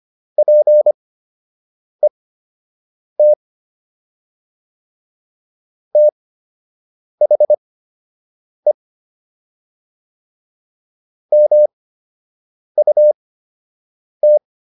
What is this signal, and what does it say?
Morse code, heard as a keyed tone, about 25 wpm, spaced as for 7 wpm, 600 Hz. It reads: PET THE MUT